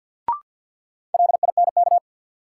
Morse code, heard as a keyed tone, about 53 wpm, 700 Hz. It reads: LINK
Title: Sentences from Top 1500 Words - New Word Drill Alt - 50wpm